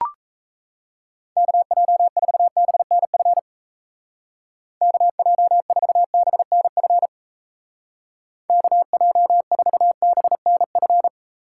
Morse code, heard as a keyed tone, about 35 wpm, 700 Hz. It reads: KJ4BNF KJ4BNF KJ4BNF
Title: Speed Racing - USA 2x3 Call Signs 28wpm